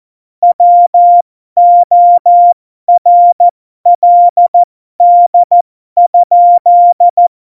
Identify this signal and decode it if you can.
Morse code, heard as a keyed tone, about 14 wpm, spaced as for 12 wpm, 700 Hz. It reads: WORLD?
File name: Morsekurs / Witze